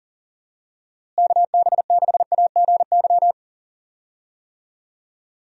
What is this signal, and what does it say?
Morse code, heard as a keyed tone, about 40 wpm, 700 Hz. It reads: KB6AGY
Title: USA Call Signs - 6 Characters - Repeated 3x - 40wpm